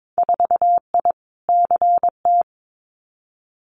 Morse code, heard as a keyed tone, about 22 wpm, 700 Hz. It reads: 4I CT
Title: Winter Field Day Exchanges Encore 22wpm